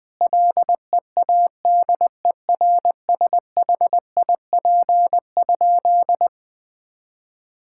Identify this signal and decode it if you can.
Morse code, heard as a keyed tone, about 20 wpm, 700 Hz. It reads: LEADERSHIP?
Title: Sentences from Top 1600 Words 20wpm